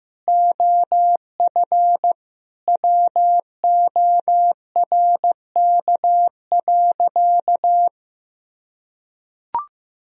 Morse code, heard as a keyed tone, about 15 wpm, 700 Hz. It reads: OF WORK.
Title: Sentences from Top 1000 Words - Review Part 2 - 15wpm